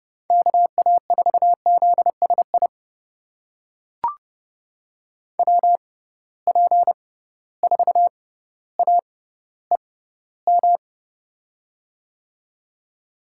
Morse code, heard as a keyed tone, about 30 wpm, spaced as for 15 wpm, 700 Hz. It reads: KA4ZSI WP4AEM
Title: USA Call Signs - 6 Characters - 6x Character Spacing - 30wpm